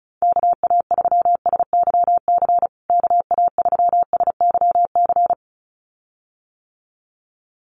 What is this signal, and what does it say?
Morse code, heard as a keyed tone, about 34 wpm, 700 Hz. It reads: KA3SYC KA3SYC
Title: USA Call Signs Repeated 2x 35wpm